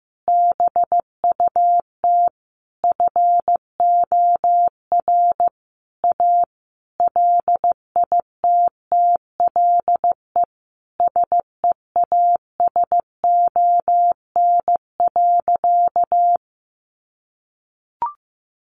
Morse code, heard as a keyed tone, about 15 wpm, 700 Hz. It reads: BUT FOR A LITTLE SEASON.
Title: Sentences from Top 1400 Words 15wpm